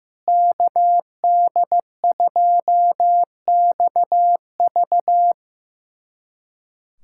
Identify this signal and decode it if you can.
Morse code, heard as a keyed tone, about 15 wpm, 700 Hz. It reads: KD2XV
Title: USA Callsign DE Callsign - 5 Characters - 15wpm